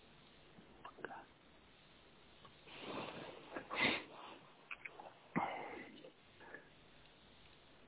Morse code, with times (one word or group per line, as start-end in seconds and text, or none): none